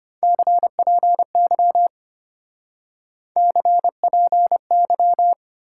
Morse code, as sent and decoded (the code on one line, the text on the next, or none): -.-. .--. -.-- / -.-. .--. -.--
CPY CPY